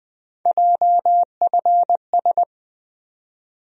Morse code, heard as a keyed tone, about 20 wpm, 700 Hz. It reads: JFS
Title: USA Call Sign Postfixes - 3 Letters 20wpm